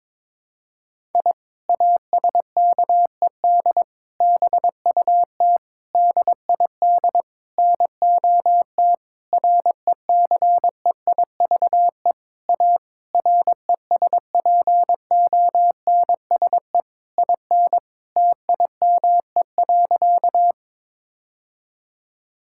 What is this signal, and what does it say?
Morse code, heard as a keyed tone, about 21 wpm, 700 Hz. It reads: I ASKED BUT DID NOT RECEIVE A RESPONSE IN TIME.
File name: practice / 22wpm / Sentences with 8-Word Phrases - 22wpm